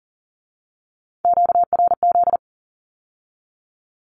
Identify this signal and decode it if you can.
Morse code, heard as a keyed tone, about 41 wpm, 700 Hz. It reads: QRZ
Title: Q-Codes with Spoken Meaning - 40wpm